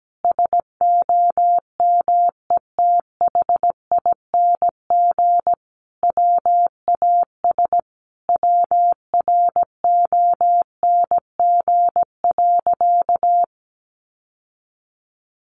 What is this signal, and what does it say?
Morse code, heard as a keyed tone, about 17 wpm, 700 Hz. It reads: SOMETHING WAS WRONG.
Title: Sentences from Top 1200 Words 17wpm